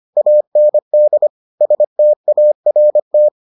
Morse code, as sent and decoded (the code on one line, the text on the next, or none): .- -. -.. / ... - .- .-. -
AND START